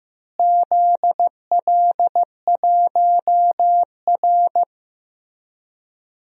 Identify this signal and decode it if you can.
Morse code, heard as a keyed tone, about 15 wpm, 700 Hz. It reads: ZL1R
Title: DXCC-100 Call Signs - 4 Characters - Repeated 3x - 15wpm